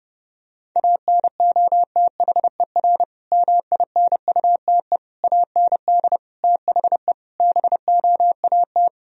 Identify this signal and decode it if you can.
Morse code, heard as a keyed tone, about 30 wpm, 700 Hz. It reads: ANOTHER MINUTE AND THE BOAT
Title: Sentences from Top 1000 Words - Review Part 3 - 30wpm